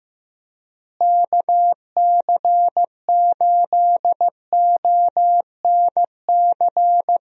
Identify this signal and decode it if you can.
Morse code, heard as a keyed tone, about 15 wpm, 700 Hz. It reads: KC8ONC
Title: USA Call Signs - 6 Characters - Repeated 3x - 15wpm